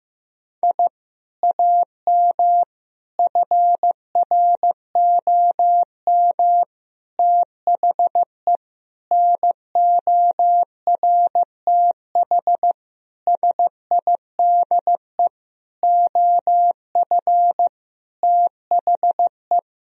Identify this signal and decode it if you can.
Morse code, heard as a keyed tone, about 16 wpm, 700 Hz. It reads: I AM FROM THE NORTH SIDE OF THE
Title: Sentences from Top 1000 Words - Review Part 1 - 15wpm